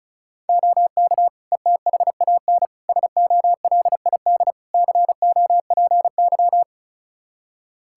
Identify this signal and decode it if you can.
Morse code, heard as a keyed tone, about 36 wpm, 700 Hz. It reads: OK ETHAN SOLID COPY
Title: Practice QSO - Exchange 3 - More Info - 35wpm